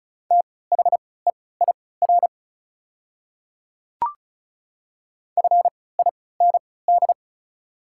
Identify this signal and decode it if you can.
Morse code, heard as a keyed tone, about 36 wpm, spaced as for 20 wpm, 690 Hz. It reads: THEIR FIND
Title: 3x Character Spacing - Top 100 Words - 35wpm